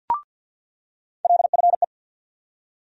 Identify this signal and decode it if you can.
Morse code, heard as a keyed tone, about 47 wpm, 700 Hz. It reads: LFE
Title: USA Call Sign Postfixes - 3 Letters 50wpm